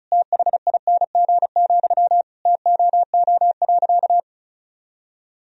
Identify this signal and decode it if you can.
Morse code, heard as a keyed tone, about 34 wpm, 700 Hz. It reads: THING, TOO.
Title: Sentences from Top 200 Words and Abbreviations 35wpm